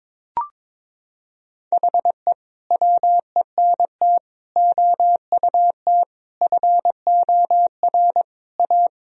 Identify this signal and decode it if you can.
Morse code, heard as a keyed tone, about 22 wpm, 700 Hz. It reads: HE WENT OUT FOR A
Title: Sentences from Top 1000 Words - Review Part 1 - 22wpm